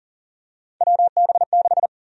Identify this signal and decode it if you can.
Morse code, heard as a keyed tone, about 38 wpm, 700 Hz. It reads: WB6
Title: USA Call Sign Prefixes - 40wpm